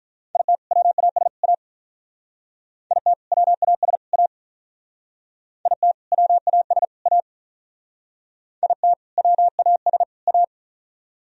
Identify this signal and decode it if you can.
Morse code, heard as a keyed tone, about 44 wpm, 700 Hz. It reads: IT WAS A IT WAS A IT WAS A IT WAS A